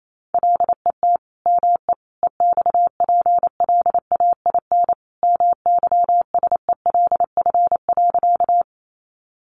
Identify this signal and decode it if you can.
Morse code, heard as a keyed tone, about 28 wpm, 700 Hz. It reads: LET ME EXPLAIN MYSELF.
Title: Sentences from Top 1300 Words 28wpm